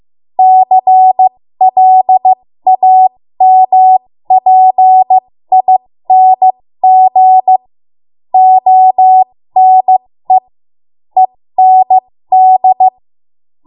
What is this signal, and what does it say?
Morse code, heard as a keyed tone, about 15 wpm, spaced as for 13 wpm, 750 Hz. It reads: CLAMPING ONE END
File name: Morse / 13WPM